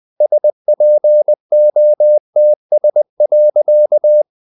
Morse code, heard as a keyed tone, about 20 wpm, 600 Hz. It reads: SPOTS.
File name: the old man and the sea part 2 20wpm